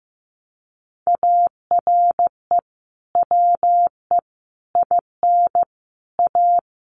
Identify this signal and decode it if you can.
Morse code, heard as a keyed tone, about 15 wpm, 700 Hz. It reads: ARE WE IN A
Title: Sentences from Top 1300 Words 15wpm